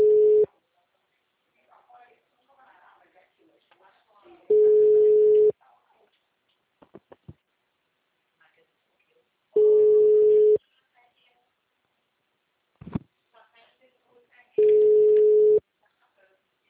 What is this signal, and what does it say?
Ringback tone (Europe, 425 Hz), with speech.